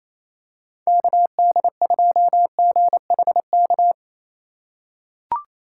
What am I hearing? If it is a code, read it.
Morse code, heard as a keyed tone, about 28 wpm, 700 Hz. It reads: KD2GHK
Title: USA Call Signs - 6 Characters - 8x Character Spacing - 28wpm